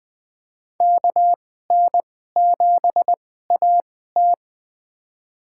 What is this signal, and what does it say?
Morse code, heard as a keyed tone, about 20 wpm, spaced as for 15 wpm, 700 Hz. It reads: KN7AT
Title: USA Call Signs - 5 Characters - 2x Character Spacing - 20wpm